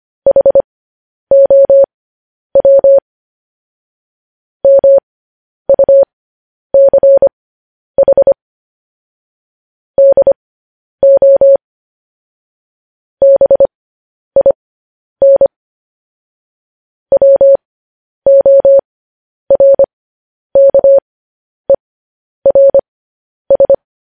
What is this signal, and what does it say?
Morse code, heard as a keyed tone, about 25 wpm, spaced as for 10 wpm, 550 Hz. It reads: HOW MUCH DO BIN WORKERS
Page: news headlines in morse code at 10 wpm